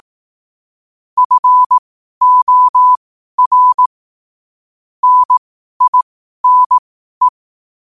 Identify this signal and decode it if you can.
Morse code, heard as a keyed tone, about 18 wpm, spaced as for 12 wpm, 1000 Hz. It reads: FOR NINE
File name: qotdfiles / 12wpm